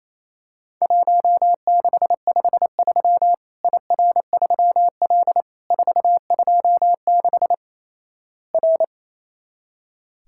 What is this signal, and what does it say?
Morse code, heard as a keyed tone, about 28 wpm, 700 Hz. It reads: 1653 IR3L 426 R
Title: Worked All Europe DX Contest - QTC Exchange - 28wpm